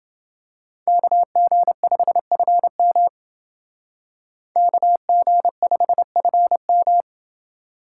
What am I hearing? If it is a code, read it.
Morse code, heard as a keyed tone, about 28 wpm, 700 Hz. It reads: KG5FM KG5FM